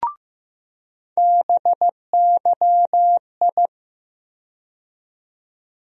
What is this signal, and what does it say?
Morse code, heard as a keyed tone, about 15 wpm, 700 Hz. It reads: BYI
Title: USA Call Sign Postfixes - 3 Letters 15wpm